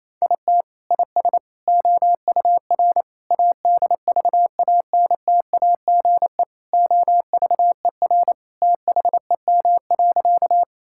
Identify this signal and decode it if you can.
Morse code, heard as a keyed tone, about 27 wpm, 700 Hz. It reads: IT IS OUR ADVANTAGE OVER THEM.
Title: Sentences from Top 1000 Words - Review Part 6 - 28wpm